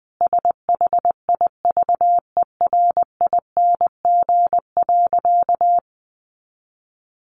Morse code, heard as a keyed tone, about 20 wpm, 700 Hz. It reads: SHIVERING.
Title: A Sound Of Thunder 20wpm